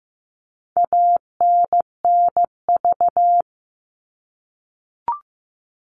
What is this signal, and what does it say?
Morse code, heard as a keyed tone, about 15 wpm, 700 Hz. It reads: ANNV
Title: All Cut Numbers - 4-Digit 15wpm